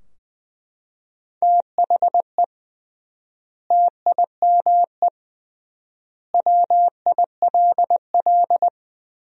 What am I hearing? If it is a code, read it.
Morse code, heard as a keyed tone, about 20 wpm, 700 Hz. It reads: THE TIME WILL
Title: 3x Word-Spacing - Sentences from Top 100 Words - 20wpm